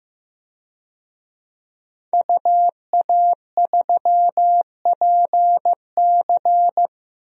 Morse code, heard as a keyed tone, about 16 wpm, 700 Hz. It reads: UA3PC
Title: DXCC-100 Call Signs - 5 Characters - Repeated 3x - 15wpm